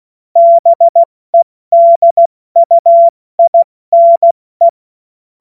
Morse code, heard as a keyed tone, about 16 wpm, 680 Hz. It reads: BEDUINE
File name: Morsekurs / Witze / Witz 28.November 2023 70 F80